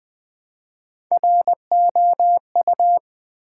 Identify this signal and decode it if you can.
Morse code, heard as a keyed tone, about 20 wpm, 700 Hz. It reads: ROU
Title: Build A Word - 1 Letter At A Time - 5-Letter Words - 20wpm